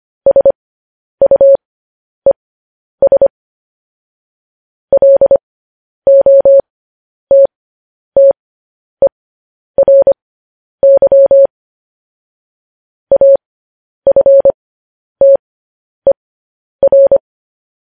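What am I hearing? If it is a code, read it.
Morse code, heard as a keyed tone, about 25 wpm, spaced as for 10 wpm, 550 Hz. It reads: SUES LOTTERY AFTER
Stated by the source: Morse code news headlines practice transmission.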